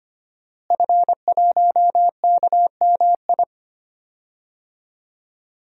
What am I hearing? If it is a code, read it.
Morse code, heard as a keyed tone, about 25 wpm, 700 Hz. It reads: F1KMI
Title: DXCC-100 Call Signs - 5 Characters - Repeated 2x - 25wpm